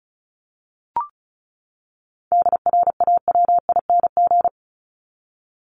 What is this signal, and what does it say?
Morse code, heard as a keyed tone, about 34 wpm, 700 Hz. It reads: DRAWING
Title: ING Suffix - Top 100 Words with Base Words 35wpm